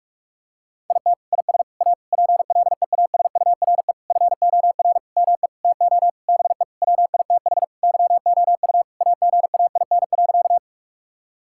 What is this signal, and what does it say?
Morse code, heard as a keyed tone, about 46 wpm, 700 Hz. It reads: IT IS A PLEASURE FOR ME TO BE WITH YOU AGAIN.
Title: Sentences from Top 1600 Words 45wpm